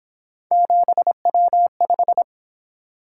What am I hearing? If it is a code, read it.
Morse code, heard as a keyed tone, about 28 wpm, 700 Hz. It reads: 7W5